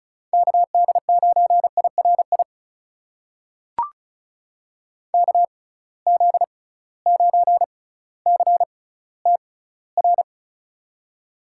Morse code, heard as a keyed tone, about 36 wpm, spaced as for 18 wpm, 700 Hz. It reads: KD9IRI KZ9CTR
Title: USA Call Signs - 6 Characters - 6x Character Spacing - 35wpm